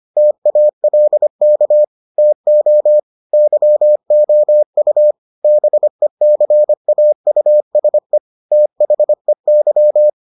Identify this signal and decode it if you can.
Morse code, heard as a keyed tone, about 25 wpm, 600 Hz. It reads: TALK TO YOU BECAUSE THEY